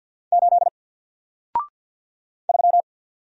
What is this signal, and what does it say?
Morse code, heard as a keyed tone, about 49 wpm, 700 Hz. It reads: 8 3
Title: Single Letter-Number Alt 50wpm